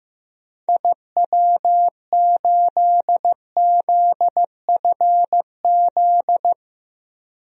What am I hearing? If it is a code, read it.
Morse code, heard as a keyed tone, about 15 wpm, 700 Hz. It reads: IW8ZFZ
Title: DXCC-100 Call Signs - 6 Characters - Repeated 3x - 15wpm